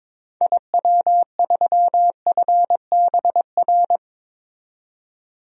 Morse code, heard as a keyed tone, about 22 wpm, 700 Hz. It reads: IW3FBR
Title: DXCC-100 Call Signs - 6 Characters - 22wpm